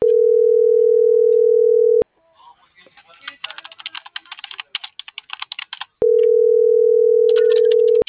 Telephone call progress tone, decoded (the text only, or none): ringback tone